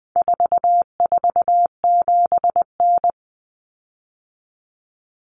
Morse code, heard as a keyed tone, about 20 wpm, 700 Hz. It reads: 447N
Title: Common Cut Numbers - 4-Digit 20wpm